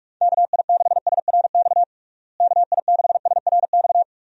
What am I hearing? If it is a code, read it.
Morse code, heard as a keyed tone, about 46 wpm, 700 Hz. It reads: KI6SRX KI6SRX
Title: USA Call Signs - 6 Characters - Repeated 2x - 45wpm